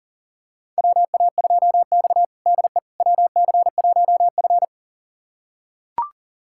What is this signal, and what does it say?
Morse code, heard as a keyed tone, about 40 wpm, 700 Hz. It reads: WA2X DE WC1F
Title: USA Callsign DE Callsign - 4 Characters - 40wpm